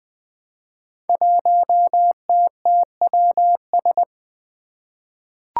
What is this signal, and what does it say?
Morse code, heard as a keyed tone, about 20 wpm, 700 Hz. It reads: 1TTWS
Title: Five Challenging Letter-Number Combinations 20wpm